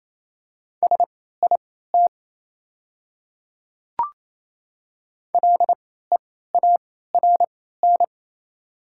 Morse code, heard as a keyed tone, about 28 wpm, spaced as for 16 wpm, 700 Hz. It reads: SIT LEARN
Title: 3x Character Spacing - Top 500 Words - 28wpm